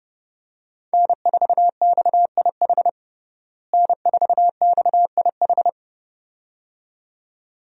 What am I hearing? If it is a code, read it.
Morse code, heard as a keyed tone, about 32 wpm, 700 Hz. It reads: N4XIH N4XIH
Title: USA Call Signs - 5 Characters - Repeated 2x - 30wpm